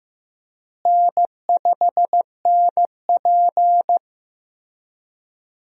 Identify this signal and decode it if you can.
Morse code, heard as a keyed tone, about 15 wpm, 700 Hz. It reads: N5NP
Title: USA Call Signs - 4 Characters - Repeated 3x - 15wpm